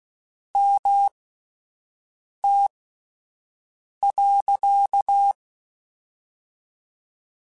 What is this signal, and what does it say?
Morse code, heard as a keyed tone, about 16 wpm, spaced as for 5 wpm, 800 Hz. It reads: MT.
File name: (spaced) / QSOs Spaced-mp3s / WD0BGP 1 @10 sp